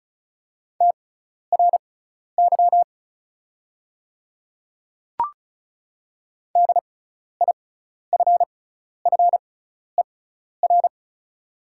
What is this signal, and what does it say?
Morse code, heard as a keyed tone, about 36 wpm, spaced as for 12 wpm, 700 Hz. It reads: TRY DIFFER
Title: 6x Character Spacing - Top 200 Words - 35wpm